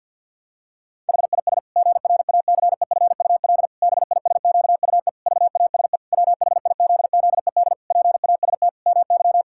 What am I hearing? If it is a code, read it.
Morse code, heard as a keyed tone, about 52 wpm, 700 Hz. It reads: HIS GRACEFUL BISQUE VASE WHIZZED PAST MY
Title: Pangram Challenge v2 - 50wpm